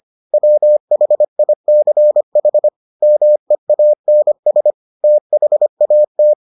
Morse code, heard as a keyed tone, about 25 wpm, 600 Hz. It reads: WHICH MEANS THAT